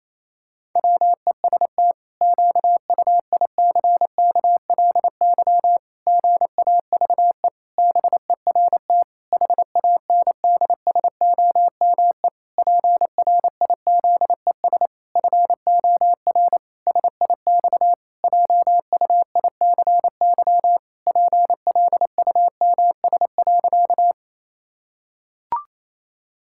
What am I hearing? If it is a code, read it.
Morse code, heard as a keyed tone, about 28 wpm, 700 Hz. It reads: WEST QUICKLY GAVE BERT HANDSOME PRIZES FOR SIX JUICY PLUMS.
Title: Pangram Challenge v2 - 28wpm